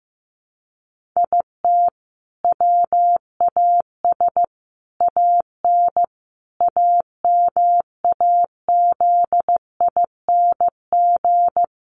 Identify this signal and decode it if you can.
Morse code, heard as a keyed tone, about 15 wpm, 700 Hz. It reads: IT WAS AN AMAZING
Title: Sentences from Top 1500 Words 15wpm